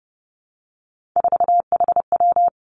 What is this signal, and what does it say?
Morse code, heard as a keyed tone, about 30 wpm, 700 Hz. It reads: 4HW
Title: Three Challenging Letter-Number Combinations 30wpm